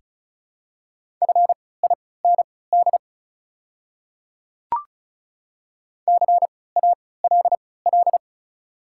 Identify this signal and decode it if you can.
Morse code, heard as a keyed tone, about 34 wpm, spaced as for 20 wpm, 700 Hz. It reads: FIND CALL
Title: 3x Character Spacing - Top 200 Words - 35wpm